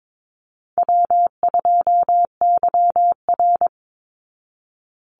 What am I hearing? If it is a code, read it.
Morse code, heard as a keyed tone, about 22 wpm, 700 Hz. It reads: W2YR
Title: Call Signs Repeated 2x - 22wpm